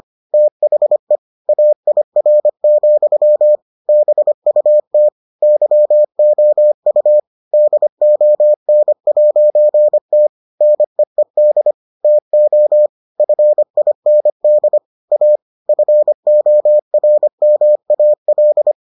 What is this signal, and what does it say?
Morse code, heard as a keyed tone, about 25 wpm, 600 Hz. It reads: THE AIR, BUT YOU DON'T NEED TO FIND A FORMAL